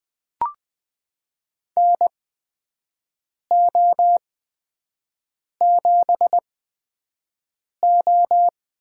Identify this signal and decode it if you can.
Morse code, heard as a keyed tone, about 20 wpm, spaced as for 5 wpm, 700 Hz. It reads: NO7O
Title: USA Call Signs - 4 Characters - 8x Character Spacing - 20wpm